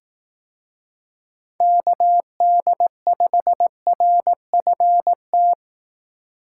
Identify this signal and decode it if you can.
Morse code, heard as a keyed tone, about 18 wpm, 700 Hz. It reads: KD5RFT
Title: Speed Racing - USA 2x3 Call Signs 15wpm